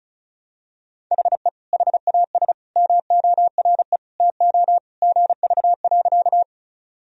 Morse code, heard as a keyed tone, about 35 wpm, 700 Hz. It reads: HE HAS MORE TO GV.
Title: Sentences from Top 200 Words and Abbreviations 35wpm